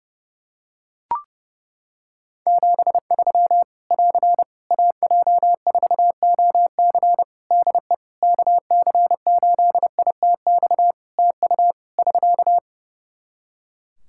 Morse code, heard as a keyed tone, about 30 wpm, 700 Hz. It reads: 73 <AR> AJ4OC DE KC8ITX TU <SK>
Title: Practice QSO - Exchange 4 - Ending - 30wpm